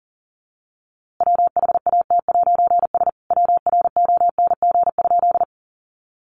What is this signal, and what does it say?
Morse code, heard as a keyed tone, about 37 wpm, 700 Hz. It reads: WHAT'S WRONG?
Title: A Sound Of Thunder 40wpm